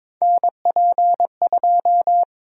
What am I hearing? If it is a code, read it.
Morse code, heard as a keyed tone, about 22 wpm, 700 Hz. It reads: NP2
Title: USA Call Sign Prefixes - 22wpm